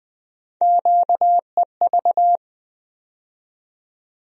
Morse code, heard as a keyed tone, about 20 wpm, 700 Hz. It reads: QEV
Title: USA Call Sign Postfixes - 3 Letters 20wpm